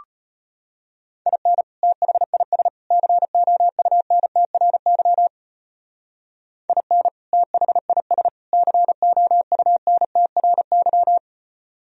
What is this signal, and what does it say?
Morse code, heard as a keyed tone, about 33 wpm, 700 Hz. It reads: IN THIS COUNTRY IN THIS COUNTRY